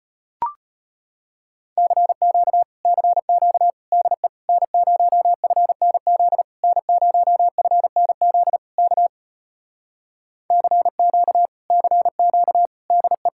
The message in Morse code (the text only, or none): CQ CQ DE N0FNZ N0FNZ K CQ CQ DE